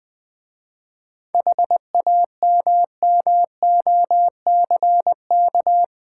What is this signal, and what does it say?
Morse code, heard as a keyed tone, about 20 wpm, 700 Hz. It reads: HAMMOCK
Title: Easy Words with Hints - 20wpm